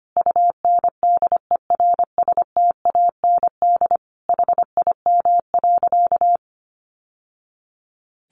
Morse code, heard as a keyed tone, about 25 wpm, 700 Hz. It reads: UNDERSTAND HIM.
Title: Sentences from Top 600 Words 25wpm